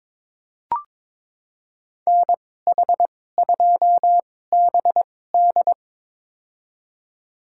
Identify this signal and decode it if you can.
Morse code, heard as a keyed tone, about 22 wpm, spaced as for 16 wpm, 700 Hz. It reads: NH2BD
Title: USA Call Signs - 5 Characters - 2x Character Spacing - 22wpm